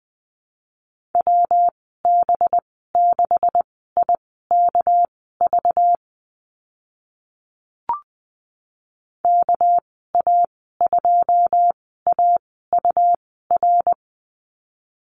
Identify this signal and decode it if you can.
Morse code, heard as a keyed tone, about 20 wpm, spaced as for 14 wpm, 700 Hz. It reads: WB6IKV KA2AUR
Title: 213 Lesson 34g - Number 8 - Call Signs 20wpm